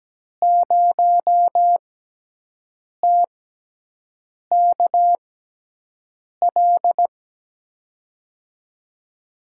Morse code, heard as a keyed tone, about 17 wpm, spaced as for 6 wpm, 700 Hz. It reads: 0TKL
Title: USA Call Signs - 6 Characters - 6x Character Spacing - 17wpm